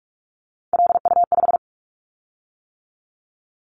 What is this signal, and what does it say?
Morse code, heard as a keyed tone, about 46 wpm, 700 Hz. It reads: LU5
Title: Callsign Prefixes for Top 100 DXCC - 45wpm